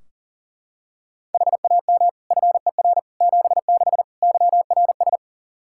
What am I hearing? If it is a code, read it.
Morse code, heard as a keyed tone, about 40 wpm, 690 Hz. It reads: HAM FER 76 YRS
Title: Practice QSO - Exchange 3 - More Info - 40wpm